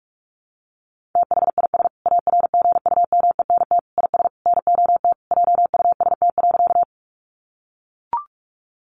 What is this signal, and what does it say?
Morse code, heard as a keyed tone, about 44 wpm, 700 Hz. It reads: THIS ARGUMENT IS NOT JUST.
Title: Sentences from Top 1400 Words 45wpm